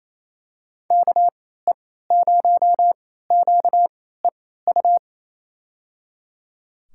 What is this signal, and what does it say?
Morse code, heard as a keyed tone, about 28 wpm, spaced as for 16 wpm, 700 Hz. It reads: KE0QEU
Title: USA Call Signs - 3x Character Spacing - 28wpm